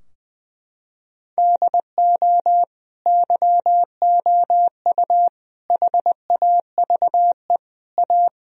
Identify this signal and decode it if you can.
Morse code, heard as a keyed tone, about 19 wpm, 700 Hz. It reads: DO YOU HAVE A
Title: Sentences from Top 1000 Words - Review Part 4 - 20wpm